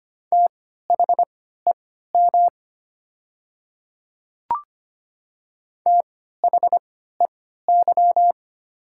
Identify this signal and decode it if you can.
Morse code, heard as a keyed tone, about 25 wpm, spaced as for 14 wpm, 700 Hz. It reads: THEM THEY
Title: 3x Character Spacing - Top 100 Words - 25wpm